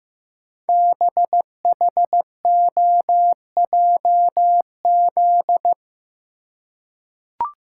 Morse code, heard as a keyed tone, about 15 wpm, 700 Hz. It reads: BHOJZ
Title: Code Groups - 5 Letters - 15wpm